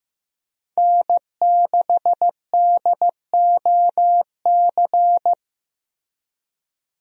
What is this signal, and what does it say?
Morse code, heard as a keyed tone, about 15 wpm, 700 Hz. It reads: N6DOC
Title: USA Call Signs - 5 Characters - Repeated 2x - 15wpm